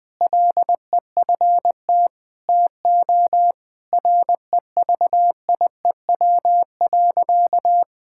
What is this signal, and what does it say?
Morse code, heard as a keyed tone, about 21 wpm, 700 Hz. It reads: LEFT TO REVIEW.
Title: Sentences from Top 1000 Words - Review Part 6 - 20wpm